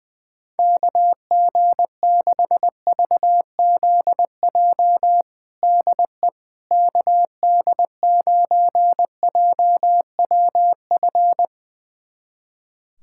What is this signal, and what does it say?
Morse code, heard as a keyed tone, about 20 wpm, 700 Hz. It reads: KG6VZJ DE KD9JWF
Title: USA Callsign DE Callsign - 6 Characters - 20wpm